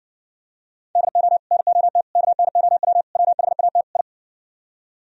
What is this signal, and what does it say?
Morse code, heard as a keyed tone, about 60 wpm, 700 Hz. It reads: DO NOT KNOW WHAT I